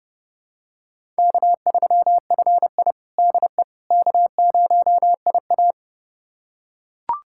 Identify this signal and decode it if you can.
Morse code, heard as a keyed tone, about 30 wpm, 700 Hz. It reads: K3FI DE K0IA
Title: USA Callsign DE Callsign - 4 Characters - 30wpm